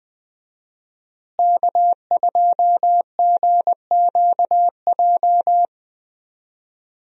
Morse code, heard as a keyed tone, about 20 wpm, 700 Hz. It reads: K2GQJ
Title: USA Call Signs - 5 Characters - Repeated 2x - 20wpm